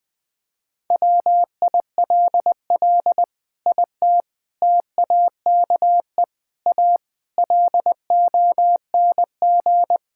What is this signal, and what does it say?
Morse code, heard as a keyed tone, about 20 wpm, 700 Hz. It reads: WILL IT TAKE A LONG
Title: Sentences from Top 1000 Words - Review Part 6 - 20wpm